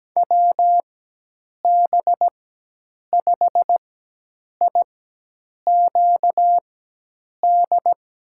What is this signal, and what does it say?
Morse code, heard as a keyed tone, about 17 wpm, spaced as for 8 wpm, 700 Hz. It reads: WB5IQD
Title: USA Call Signs - 4x Character Spacing - 17wpm